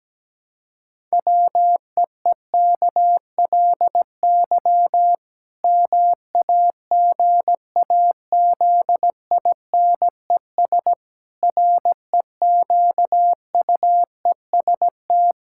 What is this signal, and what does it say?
Morse code, heard as a keyed tone, about 17 wpm, 700 Hz. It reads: WEEKLY MAGAZINES REQUEST
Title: Pangram Challenge v2 - 17wpm